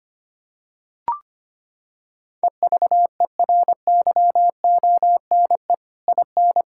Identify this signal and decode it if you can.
Morse code, heard as a keyed tone, about 25 wpm, 700 Hz. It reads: EVERYONE IN
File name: practice / 25wpm / Sentences with 3-Word Phrases - 25wpm